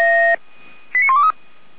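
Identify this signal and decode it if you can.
Five-tone selective call (ZVEI), heard as quick runs of five tones, tones of about 70 ms. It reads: 89123
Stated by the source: ZVEI five-tone selective call 89100 received on an SDR.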